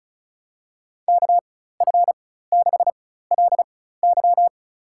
Morse code, heard as a keyed tone, about 34 wpm, spaced as for 16 wpm, 700 Hz. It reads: KF6LY
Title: USA Call Signs - 5 Characters - 4x Character Spacing - 35wpm